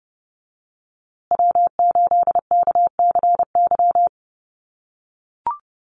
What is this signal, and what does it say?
Morse code, heard as a keyed tone, about 30 wpm, 700 Hz. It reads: W8KCY
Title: USA Call Signs Repeated 2x 30wpm